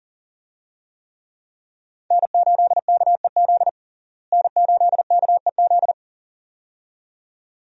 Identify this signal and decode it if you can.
Morse code, heard as a keyed tone, about 40 wpm, 700 Hz. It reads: N8KEZ N8KEZ
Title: USA Call Signs - 5 Characters - Repeated 2x - 40wpm